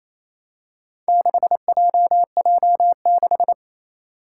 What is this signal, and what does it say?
Morse code, heard as a keyed tone, about 28 wpm, 700 Hz. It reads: 6JJ6
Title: Four Challenging Letter-Number Combinations 28wpm